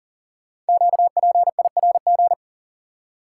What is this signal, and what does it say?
Morse code, heard as a keyed tone, about 38 wpm, 700 Hz. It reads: QPIRG
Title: Code Groups - 5 Letters - 40wpm